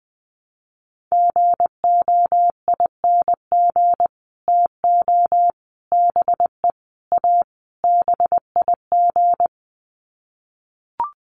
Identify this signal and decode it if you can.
Morse code, heard as a keyed tone, about 20 wpm, 700 Hz. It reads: GOING TO BE A BIG
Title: Sets of 5 Words Encore2 20wpm